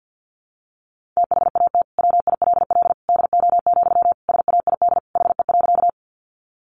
Morse code, heard as a keyed tone, about 50 wpm, 700 Hz. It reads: THAT WILL DO, SAID HE.
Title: Sentences from Top 100 Words 50wpm